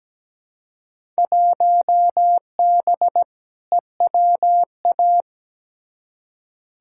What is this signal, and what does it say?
Morse code, heard as a keyed tone, about 17 wpm, 700 Hz. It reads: 1B EWA
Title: Field Day Exchanges Encore - 17wpm